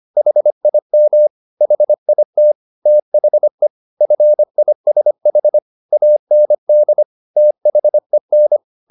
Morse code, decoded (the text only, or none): HIM HIT THE FISH AND THEN